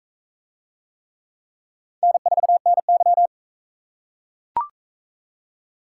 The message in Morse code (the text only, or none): N4NY